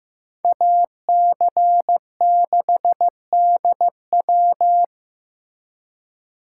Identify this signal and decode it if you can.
Morse code, heard as a keyed tone, about 18 wpm, 700 Hz. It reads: AC6DW
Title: Speed Racing - USA 2x2 Call Signs 15wpm